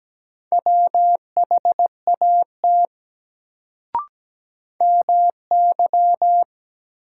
Morse code, heard as a keyed tone, about 17 wpm, 700 Hz. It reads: WHAT MY
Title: Sets of 2 Words - New Word Drill - 17wpm